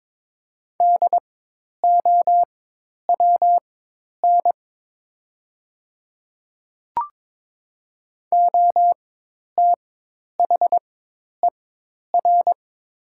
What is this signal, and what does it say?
Morse code, heard as a keyed tone, about 22 wpm, spaced as for 10 wpm, 700 Hz. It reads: DOWN OTHER
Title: 4x Character Spacing - Top 200 Words - 22wpm